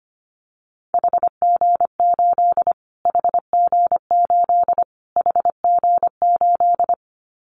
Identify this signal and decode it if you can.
Morse code, heard as a keyed tone, about 25 wpm, 700 Hz. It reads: HG8 HG8 HG8
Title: Callsign Prefixes for Top 25 DXCC - Repeated 3x - 25wpm